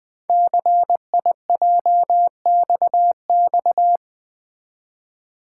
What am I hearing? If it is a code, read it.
Morse code, heard as a keyed tone, about 20 wpm, 700 Hz. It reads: CIJXX
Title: Code Groups - 5 Letters - 20wpm